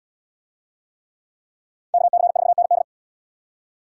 Morse code, heard as a keyed tone, about 64 wpm, 700 Hz. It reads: KC4ND